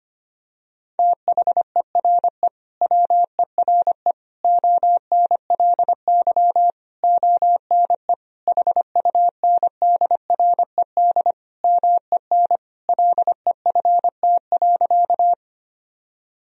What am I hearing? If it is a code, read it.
Morse code, heard as a keyed tone, about 25 wpm, 700 Hz. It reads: THERE WERE ONLY ONE HUNDRED MEN LEFT.
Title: Sentences from Top 1000 Words - Review Part 2 - 25wpm